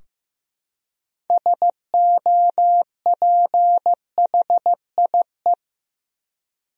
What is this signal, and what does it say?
Morse code, heard as a keyed tone, about 15 wpm, 700 Hz. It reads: SOPHIE
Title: USA Common Names - 15wpm